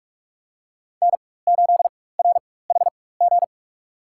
Morse code, heard as a keyed tone, about 46 wpm, spaced as for 21 wpm, 700 Hz. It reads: N8RHG
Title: USA Call Signs - 5 Characters - 4x Character Spacing - 45wpm